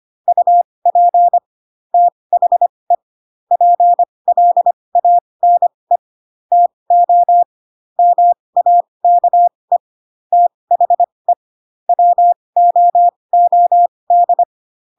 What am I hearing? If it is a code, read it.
Morse code, heard as a keyed tone, about 25 wpm, spaced as for 20 wpm, 700 Hz. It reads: UP THE PLANE TO MAKE THE WOOD
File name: Pinocchio / Pinocchio ch1 15 25 ik1hge